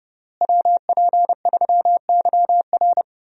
Morse code, heard as a keyed tone, about 30 wpm, 700 Hz. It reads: WP3YR
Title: USA Call Signs - 5 Characters - Repeated 2x - 30wpm